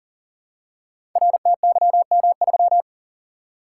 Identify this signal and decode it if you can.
Morse code, heard as a keyed tone, about 40 wpm, 700 Hz. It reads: RTYM3
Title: Code Groups - 5 Letter:Numbers - 40wpm